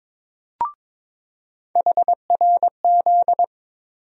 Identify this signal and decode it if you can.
Morse code, heard as a keyed tone, about 21 wpm, 700 Hz. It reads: HRZ
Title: USA Call Sign Postfixes - 3 Letters 22wpm